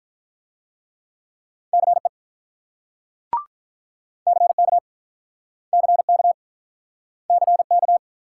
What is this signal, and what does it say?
Morse code, heard as a keyed tone, about 46 wpm, 700 Hz. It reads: CE CK CK CK